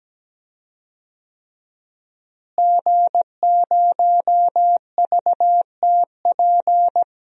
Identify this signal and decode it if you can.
Morse code, heard as a keyed tone, about 17 wpm, 700 Hz. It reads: G0VTP
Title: DXCC-100 Call Signs - 5 Characters - Repeated 3x - 17wpm